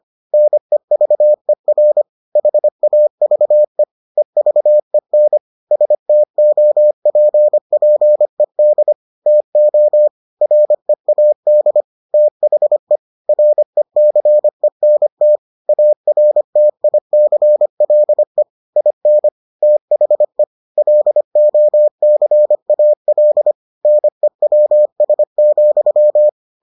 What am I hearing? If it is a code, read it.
Morse code, heard as a keyed tone, about 25 wpm, 600 Hz. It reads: NEVER HAVE EVEN STOPPED TO READ THE RECENT ARTICLE IN THE LOCAL NEWS,